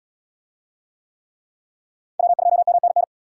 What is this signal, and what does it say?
Morse code, heard as a keyed tone, about 70 wpm, 700 Hz. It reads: W2MNN